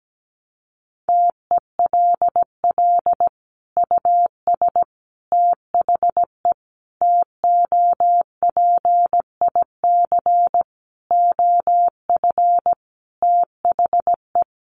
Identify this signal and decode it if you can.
Morse code, heard as a keyed tone, about 17 wpm, 700 Hz. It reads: TELL US THE TOPIC OF THE
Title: Sentences from Top 1200 Words 17wpm